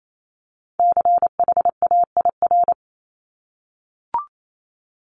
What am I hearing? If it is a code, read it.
Morse code, heard as a keyed tone, about 28 wpm, 700 Hz. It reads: CHAIR
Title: Multi-letter Phonemes 28wpm